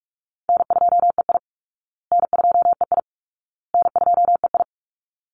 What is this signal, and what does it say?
Morse code, heard as a keyed tone, about 44 wpm, 700 Hz. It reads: N2EI N2EI N2EI
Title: Easy Call Signs Repeated 3x - 45wpm